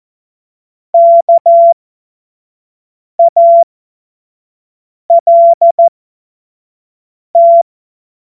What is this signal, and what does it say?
Morse code, heard as a keyed tone, about 14 wpm, spaced as for 5 wpm, 680 Hz. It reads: KALT
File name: Morsekurs / Witze / Witz 03. Mai 2022 25 F70